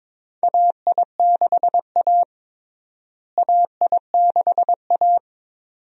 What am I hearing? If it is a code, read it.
Morse code, heard as a keyed tone, about 22 wpm, 700 Hz. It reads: AI6A AI6A
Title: USA Call Signs - 4 Characters - Repeated 2x - 22wpm